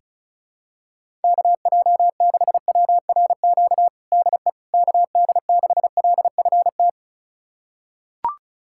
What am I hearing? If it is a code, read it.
Morse code, heard as a keyed tone, about 34 wpm, 700 Hz. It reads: KJ6WRQ DE KD6LFT
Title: USA Callsign DE Callsign - 6 Characters - 35wpm